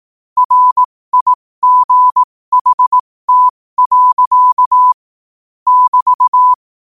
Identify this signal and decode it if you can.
Morse code, heard as a keyed tone, about 18 wpm, spaced as for 15 wpm, 1000 Hz. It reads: RIGHT. <BT>
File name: 15wpm